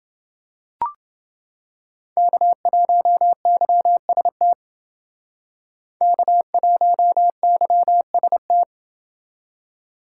Morse code, heard as a keyed tone, about 29 wpm, 700 Hz. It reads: K1YST K1YST